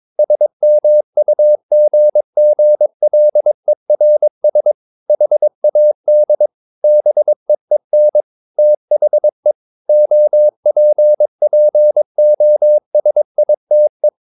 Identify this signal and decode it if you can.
Morse code, heard as a keyed tone, about 22 wpm, 600 Hz. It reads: SMUGGLERS HAD BEEN THE OPPOSITE
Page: CW med Gnister